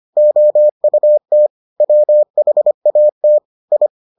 Morse code, heard as a keyed tone, about 25 wpm, 600 Hz. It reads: OUT WHAT I